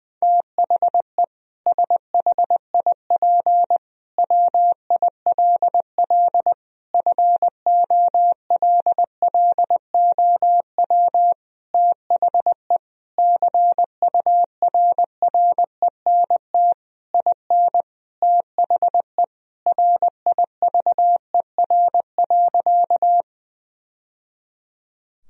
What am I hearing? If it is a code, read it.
Morse code, heard as a keyed tone, about 21 wpm, 700 Hz. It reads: THE SHIP WILL FOLLOW THE CURRENT IN THE RIVER.
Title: Sentences from Top 1000 Words - Review Part 4 - 20wpm